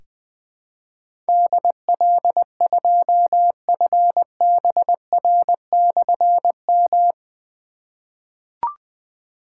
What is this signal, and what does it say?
Morse code, heard as a keyed tone, about 20 wpm, 700 Hz. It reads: DL2FBR/M
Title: Difficult Call Signs 20wpm